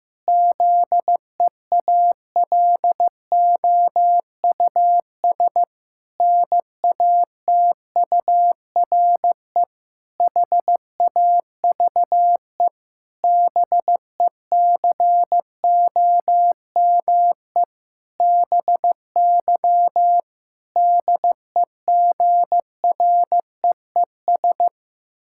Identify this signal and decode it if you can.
Morse code, heard as a keyed tone, about 15 wpm, 700 Hz. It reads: ZEALOUS NATURE HAVE BECOME BY DEGREES
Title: Pangram Challenge v2 - 15wpm